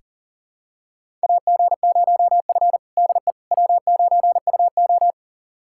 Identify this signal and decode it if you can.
Morse code, heard as a keyed tone, about 40 wpm, 700 Hz. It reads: AG0F DE W9UO
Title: USA Callsign DE Callsign - 4 Characters - 40wpm